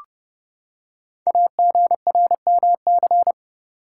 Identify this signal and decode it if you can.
Morse code, heard as a keyed tone, about 29 wpm, 700 Hz. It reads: AGRMC
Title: Code Groups - 5 Letters - 30wpm